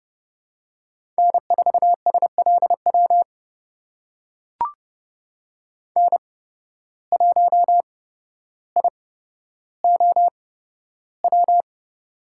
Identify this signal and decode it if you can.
Morse code, heard as a keyed tone, about 32 wpm, spaced as for 13 wpm, 700 Hz. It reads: N4SLW N1IOW
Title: USA Call Signs - 5 Characters - 8x Character Spacing - 30wpm